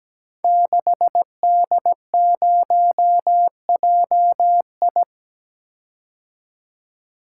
Morse code, heard as a keyed tone, about 17 wpm, 700 Hz. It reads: 6D0JI
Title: Code Groups - 5 Letter:Numbers - 17wpm